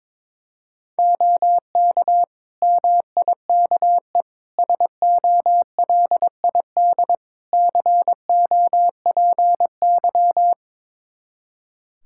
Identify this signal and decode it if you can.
Morse code, heard as a keyed tone, about 22 wpm, 700 Hz. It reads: OK MIKE SOLID COPY
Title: Practice QSO - Exchange 3 - More Info - 22wpm